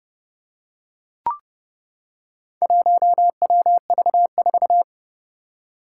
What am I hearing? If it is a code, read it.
Morse code, heard as a keyed tone, about 30 wpm, 700 Hz. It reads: 1WV4
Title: Four Challenging Letter-Number Combinations 30wpm